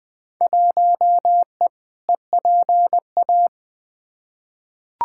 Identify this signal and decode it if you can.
Morse code, heard as a keyed tone, about 20 wpm, 700 Hz. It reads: 1E EPA
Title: Field Day Exchanges - 20wpm